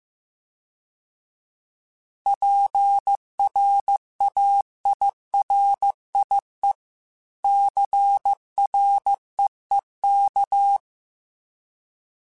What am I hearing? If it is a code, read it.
Morse code, heard as a keyed tone, about 15 wpm, 800 Hz. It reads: PRAIRIE CREEK
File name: fromSMs / FromSMs mp3s / Cities@13